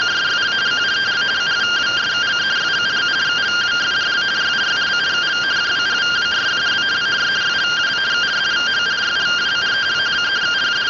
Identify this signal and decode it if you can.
Two-tone teleprinter signal, about 170 Hz shift.